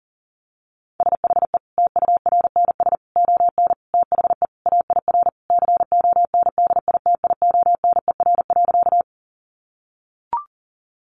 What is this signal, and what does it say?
Morse code, heard as a keyed tone, about 40 wpm, 700 Hz. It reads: SHE TURNS ON THE AIR CONDITIONER.
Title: Easy Phrases 40wpm